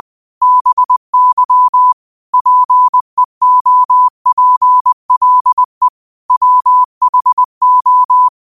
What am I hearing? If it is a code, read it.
Morse code, heard as a keyed tone, about 20 wpm, 1000 Hz. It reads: BY PEOPLE WHO
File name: qotdfiles / 20wpm